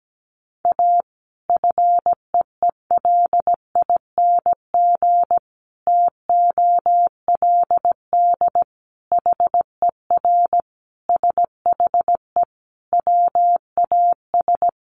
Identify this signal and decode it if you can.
Morse code, heard as a keyed tone, about 17 wpm, 700 Hz. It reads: A FEELING TOLD HER SHE WAS
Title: Sentences from Top 1200 Words 17wpm